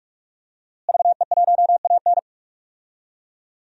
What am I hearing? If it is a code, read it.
Morse code, heard as a keyed tone, about 48 wpm, 700 Hz. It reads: VE1AN
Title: DXCC-100 Call Signs - 5 Characters - Repeated 2x - 45wpm